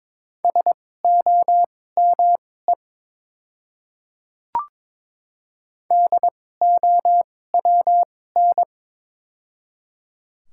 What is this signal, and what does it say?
Morse code, heard as a keyed tone, about 22 wpm, spaced as for 16 wpm, 700 Hz. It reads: SOME DOWN
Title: 2x Character Spacing - Top 200 Words - 22wpm